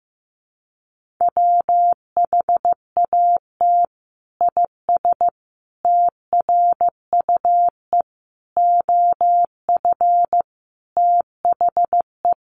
Morse code, heard as a keyed tone, about 15 wpm, 700 Hz. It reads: WHAT IS TRUE OF THE
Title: Sentences from Top 1400 Words 15wpm